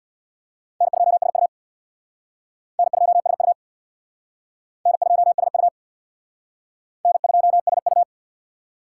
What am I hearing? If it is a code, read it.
Morse code, heard as a keyed tone, about 66 wpm, 700 Hz. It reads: N2SU N2SU N2SU N2SU